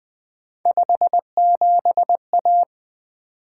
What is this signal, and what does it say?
Morse code, heard as a keyed tone, about 20 wpm, 700 Hz. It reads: 57A
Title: Three Letter-Number Combinations - 20wpm